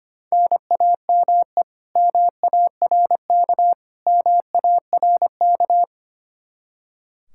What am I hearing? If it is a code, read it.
Morse code, heard as a keyed tone, about 25 wpm, 700 Hz. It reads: NAME MARK MARK
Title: Practice QSO - Exchange 1 - Beginning - 25wpm